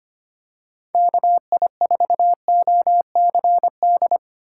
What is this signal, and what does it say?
Morse code, heard as a keyed tone, about 25 wpm, 700 Hz. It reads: KI4OCD
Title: USA Call Signs - 6 Characters - 4x Character Spacing - 25wpm